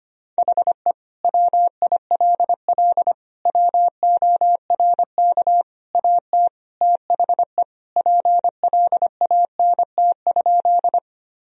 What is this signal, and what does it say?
Morse code, heard as a keyed tone, about 25 wpm, 700 Hz. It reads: HE WILL WORK AT THE PLANT?
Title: Sentences from Top 1000 Words - Review Part 1 - 25wpm